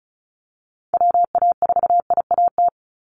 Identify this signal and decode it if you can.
Morse code, heard as a keyed tone, about 34 wpm, 700 Hz. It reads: WA4IAT
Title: USA Call Signs 35wpm